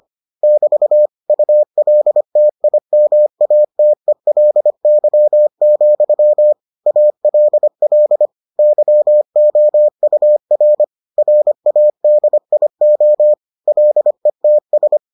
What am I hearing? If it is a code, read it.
Morse code, heard as a keyed tone, about 25 wpm, 600 Hz. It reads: <BT> ULTIMATELY, ALL YOUR RADIO LETS